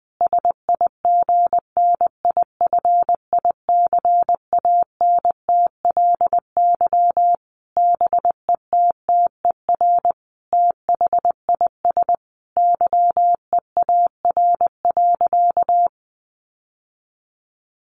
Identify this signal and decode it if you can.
Morse code, heard as a keyed tone, about 20 wpm, 700 Hz. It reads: SIGNIFICANTLY BETTER THIS YEAR.
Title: Sentences from Top 1500 Words 20wpm